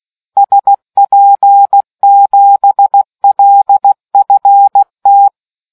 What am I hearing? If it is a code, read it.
Morse code, heard as a keyed tone, about 16 wpm, 800 Hz. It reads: SP7LFT